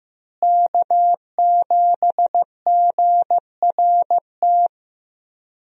Morse code, heard as a keyed tone, about 15 wpm, 700 Hz. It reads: K7GRT
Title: USA Call Signs - 5 Characters - Repeated 3x - 15wpm